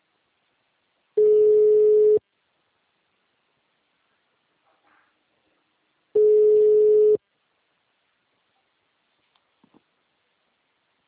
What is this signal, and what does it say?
Ringback tone (Europe, 425 Hz).
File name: out- -2018- - - .28794